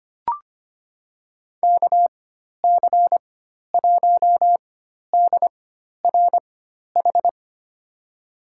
Morse code, heard as a keyed tone, about 25 wpm, spaced as for 12 wpm, 700 Hz. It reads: KC1DRH
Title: USA Call Signs - 6 Characters - 4x Character Spacing - 25wpm